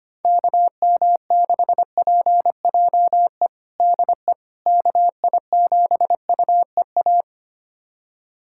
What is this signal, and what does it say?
Morse code, heard as a keyed tone, about 25 wpm, 700 Hz. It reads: KM6PJE DE KI7UEA
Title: USA Callsign DE Callsign 25wpm